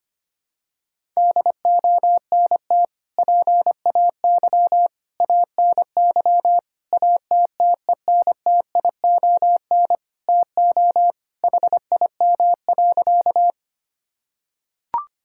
Morse code, heard as a keyed tone, about 25 wpm, 700 Hz. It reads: DONT PAY ANY ATTENTION TO HIM.
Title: Sentences from Top 1000 Words - Review Part 6 - 25wpm